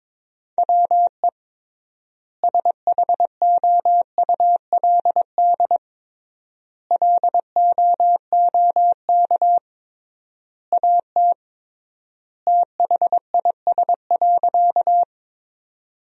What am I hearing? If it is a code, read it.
Morse code, heard as a keyed tone, about 22 wpm, 700 Hz. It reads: WE SHOULD LOOK AT THIS.
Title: 3x Word-Spacing - Sentences from Top 300 Words - 22wpm